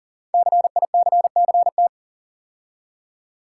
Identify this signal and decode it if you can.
Morse code, heard as a keyed tone, about 40 wpm, 700 Hz. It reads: CICCT
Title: Code Groups - 5 Letters - 40wpm